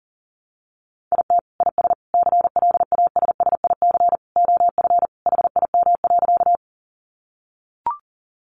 Morse code, heard as a keyed tone, about 40 wpm, 700 Hz. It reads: IT IS CLASSIC OF HIM.
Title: Sentences from Top 1000 Words 40wpm